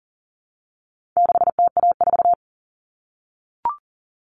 Morse code, heard as a keyed tone, about 40 wpm, 700 Hz. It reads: 6TA4
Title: All Cut Numbers - 4-Digit 40wpm